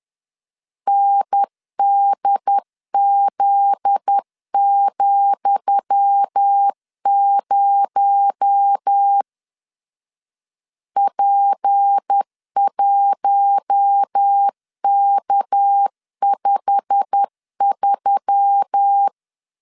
Morse code, heard as a keyed tone, about 10 wpm, 800 Hz. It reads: NDZ,0 P1K53
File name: LEZIONI / 10 SIGNS / 10D COMA 10wpm 3013